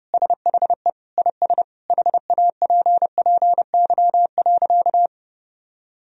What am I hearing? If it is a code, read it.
Morse code, heard as a keyed tone, about 30 wpm, 700 Hz. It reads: SHE IS HAPPY.
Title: Sentences from Top 1000 Words - Review Part 3 - 30wpm